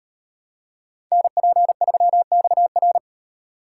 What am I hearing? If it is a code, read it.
Morse code, heard as a keyed tone, about 36 wpm, 700 Hz. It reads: NP3XR